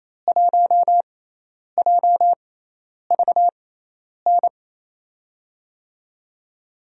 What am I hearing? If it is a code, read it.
Morse code, heard as a keyed tone, about 28 wpm, spaced as for 10 wpm, 700 Hz. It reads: 1JVN
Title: USA Call Signs - 6 Characters - 6x Character Spacing - 28wpm